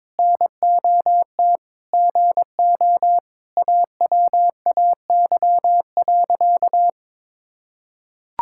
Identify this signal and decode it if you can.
Morse code, heard as a keyed tone, about 22 wpm, 700 Hz. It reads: NOT GO AWAY.
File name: practice / 22wpm / Sentences with 4-Word Phrases - 22wpm